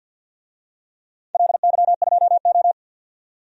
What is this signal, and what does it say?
Morse code, heard as a keyed tone, about 51 wpm, 700 Hz. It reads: LY2O
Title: DXCC-100 Call Signs - 4 Characters - 50wpm